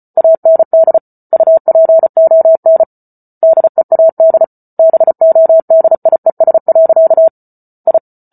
Morse code, heard as a keyed tone, about 34 wpm, 650 Hz. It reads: AND UPON DEAD BODIES. I